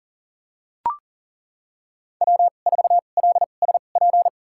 Morse code, heard as a keyed tone, about 40 wpm, spaced as for 29 wpm, 700 Hz. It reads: W4LSP
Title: USA Call Signs - 2x Character Spacing - 40wpm